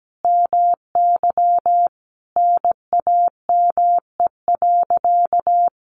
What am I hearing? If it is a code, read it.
Morse code, heard as a keyed tone, about 17 wpm, 700 Hz. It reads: MY NAME.
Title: Sentences from Top 500 Words 17wpm